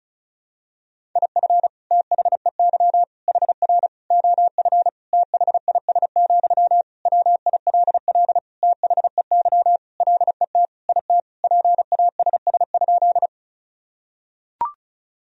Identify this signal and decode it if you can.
Morse code, heard as a keyed tone, about 36 wpm, 700 Hz. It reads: IF THEY HR OF THIS, WILL THEY LET IT PASS?
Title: Sentences from Top 400 Words and Abbreviations 35wpm